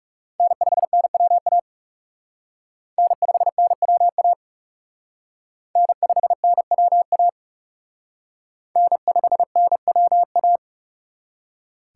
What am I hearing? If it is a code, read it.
Morse code, heard as a keyed tone, about 36 wpm, 700 Hz. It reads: N5NWA N5NWA N5NWA N5NWA
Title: Speed Racing - USA Call Signs 30wpm